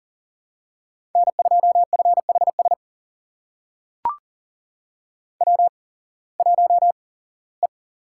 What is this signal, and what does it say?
Morse code, heard as a keyed tone, about 40 wpm, spaced as for 21 wpm, 700 Hz. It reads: N2FHS W1E
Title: USA Call Signs - 5 Characters - 8x Character Spacing - 40wpm